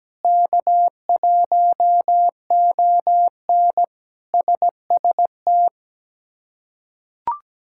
Morse code, heard as a keyed tone, about 17 wpm, 700 Hz. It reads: K1ON SST
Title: K1USN SST QSO Practice - 17wpm